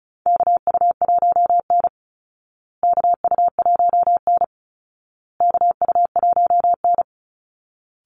Morse code, heard as a keyed tone, about 34 wpm, 700 Hz. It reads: KU1N KU1N KU1N
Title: Call Signs Repeated 2x - 35wpm